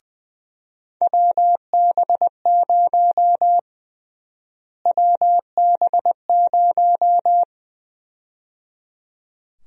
Morse code, heard as a keyed tone, about 20 wpm, 700 Hz. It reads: WB0 WB0
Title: USA Call Sign Prefixes - Repeated 2x - 20wpm